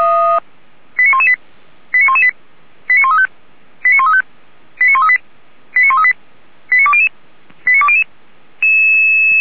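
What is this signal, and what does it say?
Five-tone selective call (ZVEI), heard as quick runs of five tones, tones of about 70 ms. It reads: 89118 89118 89135 89135 89138 89138 89200 89200
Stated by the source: ZVEI five-tone call 89227 received on an SDR, channel 464, from Feuerwehr Orlamünde.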